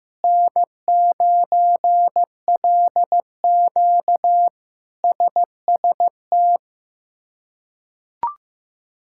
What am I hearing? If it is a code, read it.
Morse code, heard as a keyed tone, about 15 wpm, 700 Hz. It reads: N9LQ SST
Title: K1USN SST QSO Practice - 15wpm